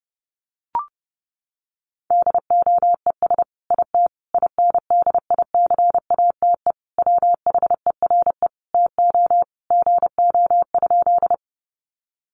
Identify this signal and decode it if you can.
Morse code, heard as a keyed tone, about 30 wpm, 700 Hz. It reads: DOES IT INDICATE WHERE TO GO?
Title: Sentences from Top 1300 Words 30wpm